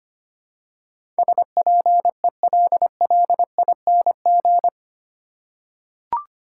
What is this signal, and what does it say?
Morse code, heard as a keyed tone, about 25 wpm, 700 Hz. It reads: SPELLING
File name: Eight Letter Words - 25wpm